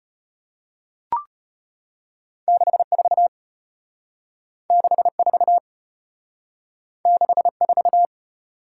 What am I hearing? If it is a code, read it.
Morse code, heard as a keyed tone, about 36 wpm, 700 Hz. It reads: 64 64 64